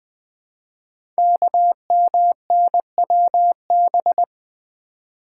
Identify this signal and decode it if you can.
Morse code, heard as a keyed tone, about 20 wpm, 700 Hz. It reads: KMNWB
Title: Code Groups - 5 Letters - 20wpm